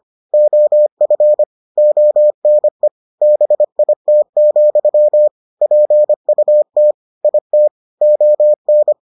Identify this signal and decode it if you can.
Morse code, heard as a keyed tone, about 25 wpm, 600 Hz. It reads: OF ONE BIT, PUT IT ON